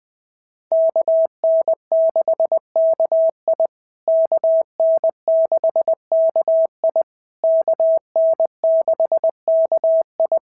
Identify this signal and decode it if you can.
Morse code, heard as a keyed tone, about 19 wpm, 650 Hz. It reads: KN6KI KN6KI KN6KI
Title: POTA QSO Practice - 20wpm